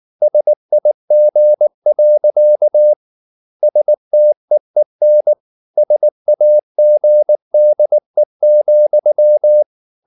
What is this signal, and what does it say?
Morse code, heard as a keyed tone, about 19 wpm, 600 Hz. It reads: SIG. STEEN SAGDE,
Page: Ferie 19 wpm